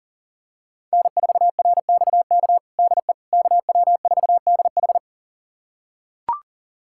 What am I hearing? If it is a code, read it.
Morse code, heard as a keyed tone, about 40 wpm, 700 Hz. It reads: N4RXK DE KW4DH
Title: USA Callsign DE Callsign - 5 Characters - 40wpm